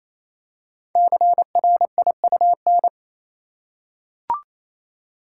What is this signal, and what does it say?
Morse code, heard as a keyed tone, about 29 wpm, 700 Hz. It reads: CRIUN
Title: Code Groups - 5 Letters - 28wpm